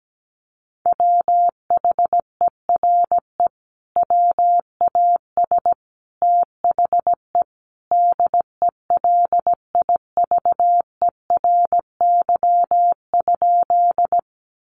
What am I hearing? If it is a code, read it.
Morse code, heard as a keyed tone, about 17 wpm, 700 Hz. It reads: WHERE WAS THE DELIVERY?
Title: Sentences from Top 1500 Words 17wpm